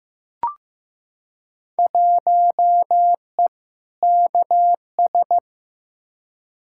Morse code, heard as a keyed tone, about 15 wpm, 700 Hz. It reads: 1E KS
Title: Field Day Exchanges - 15wpm